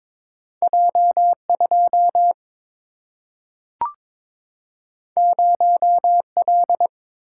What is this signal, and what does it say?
Morse code, heard as a keyed tone, about 22 wpm, 700 Hz. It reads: J 2 0 L
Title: Single-Letter Single-Number 22wpm